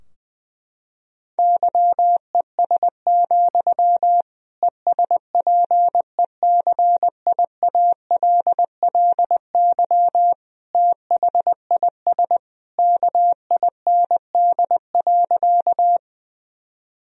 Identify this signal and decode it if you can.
Morse code, heard as a keyed tone, about 20 wpm, 700 Hz. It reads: YES, ESPECIALLY THIS KIND.
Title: Sentences from Top 1000 Words - Review Part 4 - 20wpm